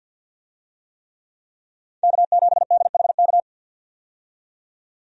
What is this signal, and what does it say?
Morse code, heard as a keyed tone, about 46 wpm, 700 Hz. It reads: K7DHK
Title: USA Call Signs - 5 Characters - 50wpm